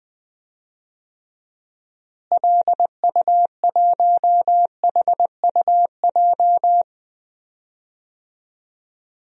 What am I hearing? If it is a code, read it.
Morse code, heard as a keyed tone, about 21 wpm, 700 Hz. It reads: LU1HUJ
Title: DXCC-100 Call Signs - 6 Characters - Repeated 3x - 20wpm